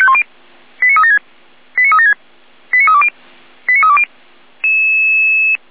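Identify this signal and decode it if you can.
Five-tone selective call (ZVEI), heard as quick runs of five tones, tones of about 70 ms. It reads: 89376 89376 89320 89320
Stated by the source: ZVEI five-tone call received by SDR.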